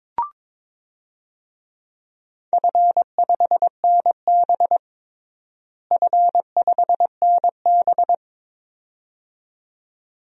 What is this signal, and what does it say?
Morse code, heard as a keyed tone, about 22 wpm, 700 Hz. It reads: F5NB F5NB
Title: DXCC-100 Call Signs - 4 Characters - Repeated 2x - 22wpm